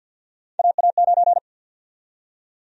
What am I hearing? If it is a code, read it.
Morse code, heard as a keyed tone, about 52 wpm, 700 Hz. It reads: AA9
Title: USA Call Sign Prefixes - 50wpm